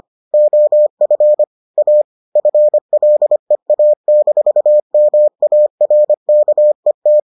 Morse code, heard as a keyed tone, about 25 wpm, 600 Hz. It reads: OF A FLEA-MARKET